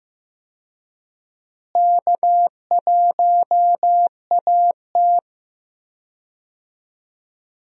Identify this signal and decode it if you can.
Morse code, heard as a keyed tone, about 15 wpm, 700 Hz. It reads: K1AT
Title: USA Call Signs - 4 Characters - Repeated 3x - 15wpm